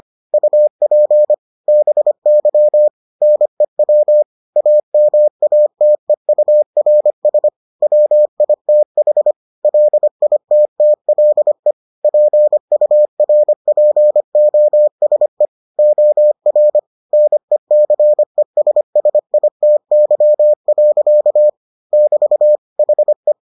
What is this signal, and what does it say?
Morse code, heard as a keyed tone, about 25 wpm, 600 Hz. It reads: UP BY NEW AMATEURS WITH LITTLE PURPOSE OR NECESSITY. <BT> HE